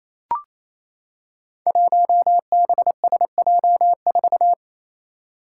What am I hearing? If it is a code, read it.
Morse code, heard as a keyed tone, about 28 wpm, 700 Hz. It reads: 1BSJ4
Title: Five Challenging Letter-Number Combinations 28wpm